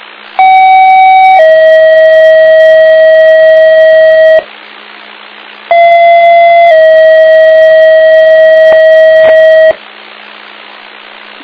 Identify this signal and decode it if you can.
Two-tone page, heard as a one-second tone, then a three-second tone, about 760 then 640 Hz, about 680 then 640 Hz. It is heard from 0.4 to 9.7 s.